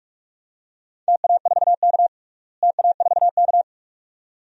Morse code, heard as a keyed tone, about 45 wpm, 700 Hz. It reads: TA4K TA4K
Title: DXCC-100 Call Signs - 4 Characters - Repeated 2x - 45wpm